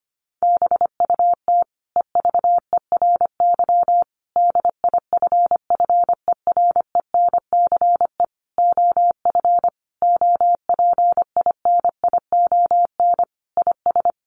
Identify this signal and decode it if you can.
Morse code, heard as a keyed tone, about 25 wpm, 700 Hz. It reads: BUT EVERY DIFFERENCE OF OPINION IS
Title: Sentences from Top 1400 Words 25wpm